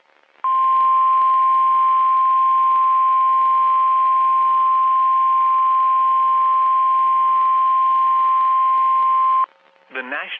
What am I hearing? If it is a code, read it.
Emergency Alert System (SAME) tones (the 1050 Hz attention tone). It is heard from 0.4 to 9.5 s.
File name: eas messages / EAS WXM31 Elmira